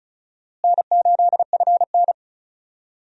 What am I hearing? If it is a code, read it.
Morse code, heard as a keyed tone, about 36 wpm, 700 Hz. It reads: N8FN
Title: USA Call Signs - 4 Characters - Repeated 2x - 35wpm